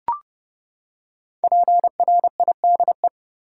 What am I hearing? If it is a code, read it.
Morse code, heard as a keyed tone, about 31 wpm, 700 Hz. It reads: PRIDE
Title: Five Letter Words - 30wpm